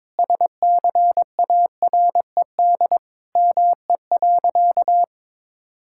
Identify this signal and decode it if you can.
Morse code, heard as a keyed tone, about 21 wpm, 700 Hz. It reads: SCARED ME.
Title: Sentences from Top 1000 Words - Review Part 5 - 22wpm